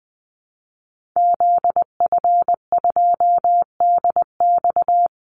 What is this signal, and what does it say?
Morse code, heard as a keyed tone, about 20 wpm, 700 Hz. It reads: ZF2DX
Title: Call Signs - 20wpm